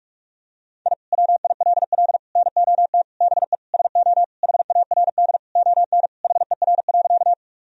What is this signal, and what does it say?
Morse code, heard as a keyed tone, about 46 wpm, 700 Hz. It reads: I WILL NOT BE SO HARD ON HER.
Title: Sentences from Top 1000 Words - Review Part 1 - 45wpm